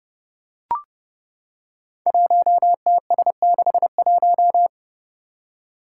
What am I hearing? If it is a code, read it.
Morse code, heard as a keyed tone, about 30 wpm, 700 Hz. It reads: 1TS61
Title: Five Challenging Letter-Number Combinations 30wpm